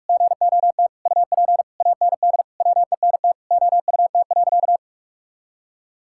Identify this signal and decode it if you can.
Morse code, heard as a keyed tone, about 41 wpm, 700 Hz. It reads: GOT UP AND WENT OUT.
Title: Sentences from Top 1600 Words 45wpm